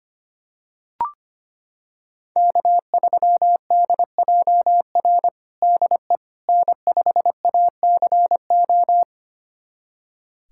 Morse code, heard as a keyed tone, about 25 wpm, 700 Hz. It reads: K3DJR DE N5ACO
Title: USA Callsign DE Callsign - 5 Characters - 25wpm